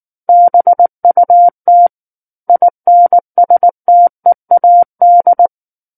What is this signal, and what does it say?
Morse code, heard as a keyed tone, about 19 wpm, 700 Hz. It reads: BUT INSTEAD
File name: Book One / 10-Chapter 10 - 19 WPM